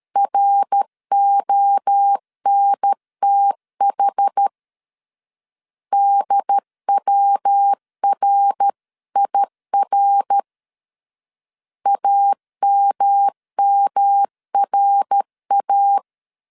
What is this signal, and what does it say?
Morse code, heard as a keyed tone, about 13 wpm, 800 Hz. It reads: RONTH DWRIR AMMRA
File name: CW / DET / 1 3 1500C S B 12wpm 2828